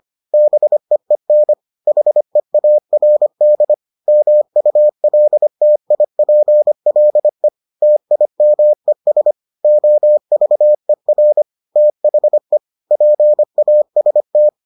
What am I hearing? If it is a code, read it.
Morse code, heard as a keyed tone, about 25 wpm, 600 Hz. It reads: BEEN HEARD MULTIPLE TIMES OVER THE PAST